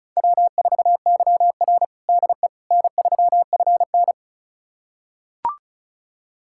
Morse code, heard as a keyed tone, about 36 wpm, 700 Hz. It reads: W4YR DE N3FN
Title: USA Callsign DE Callsign - 4 Characters - 35wpm